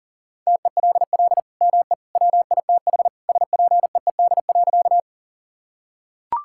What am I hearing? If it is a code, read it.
Morse code, heard as a keyed tone, about 40 wpm, 700 Hz. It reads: TELL ME WITH SPEED.
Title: Sentences from Top 1000 Words - Review Part 5 - 40wpm